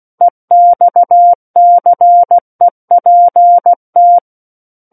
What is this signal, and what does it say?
Morse code, heard as a keyed tone, about 16 wpm, 700 Hz. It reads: EXCEPT
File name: Book One / 07-Chapter 7 - 16 WPM